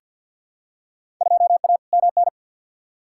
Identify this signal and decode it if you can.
Morse code, heard as a keyed tone, about 50 wpm, 700 Hz. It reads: 2A MN
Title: Field Day Exchanges - 50wpm